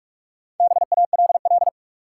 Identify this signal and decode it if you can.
Morse code, heard as a keyed tone, about 43 wpm, 700 Hz. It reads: BALL
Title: Sentences from Top 900 Words - New Word Drill - 45wpm